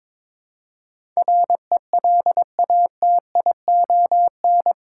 Morse code, heard as a keyed tone, about 22 wpm, 700 Hz. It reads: RELATION
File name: practice / 22wpm / Eight Letter Words - 22wpm